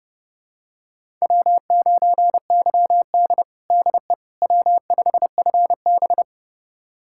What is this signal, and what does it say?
Morse code, heard as a keyed tone, about 32 wpm, 700 Hz. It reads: W9YD DE W5FB
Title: USA Callsign DE Callsign - 4 Characters - 30wpm